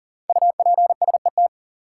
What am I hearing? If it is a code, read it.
Morse code, heard as a keyed tone, about 40 wpm, 700 Hz. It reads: UPSET
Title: Sentences from Top 800 Words - New Word Drill - 40wpm